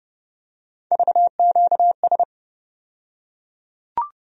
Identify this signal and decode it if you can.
Morse code, heard as a keyed tone, about 30 wpm, 700 Hz. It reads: VQS
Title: USA Call Sign Postfixes - 3 Letters 30wpm